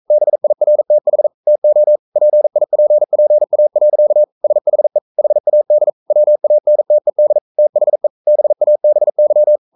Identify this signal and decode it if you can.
Morse code, heard as a keyed tone, about 42 wpm, 600 Hz. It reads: BIRTH TO PIPPA. SHE HAD WANTED THE BABY
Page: Romanen Never af Ken Follet på engelsk.